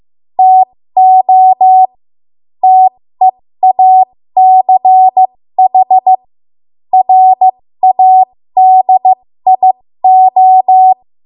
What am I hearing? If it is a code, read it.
Morse code, heard as a keyed tone, about 15 wpm, spaced as for 13 wpm, 750 Hz. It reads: TO TEACH RADIO
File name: Morse / 13WPM